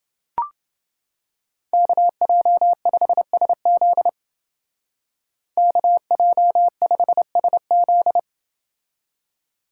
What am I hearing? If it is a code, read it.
Morse code, heard as a keyed tone, about 29 wpm, 700 Hz. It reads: KJ5SZ KJ5SZ